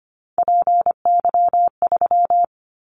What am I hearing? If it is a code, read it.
Morse code, heard as a keyed tone, about 25 wpm, 700 Hz. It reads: PY3
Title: Callsign Prefixes for Top 50 DXCC - 25wpm